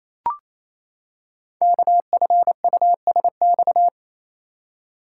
Morse code, heard as a keyed tone, about 29 wpm, 700 Hz. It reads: KFUSX
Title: Code Groups - 5 Letters - 28wpm